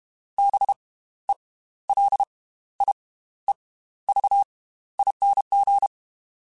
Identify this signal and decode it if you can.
Morse code, heard as a keyed tone, about 32 wpm, spaced as for 15 wpm, 800 Hz. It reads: BELIEVING
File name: Slow ing @30